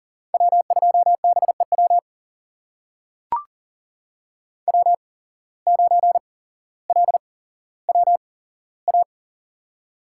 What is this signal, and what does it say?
Morse code, heard as a keyed tone, about 40 wpm, spaced as for 17 wpm, 700 Hz. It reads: W2BEW W9LWA
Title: USA Call Signs - 5 Characters - 8x Character Spacing - 40wpm